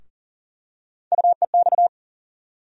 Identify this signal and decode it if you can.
Morse code, heard as a keyed tone, about 39 wpm, 700 Hz. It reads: UEX
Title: USA Call Sign Postfixes - 3 Letters 40wpm